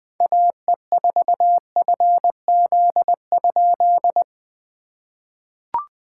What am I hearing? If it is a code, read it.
Morse code, heard as a keyed tone, about 19 wpm, 700 Hz. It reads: AE4FZ?
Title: POTA QSO Practice - 20wpm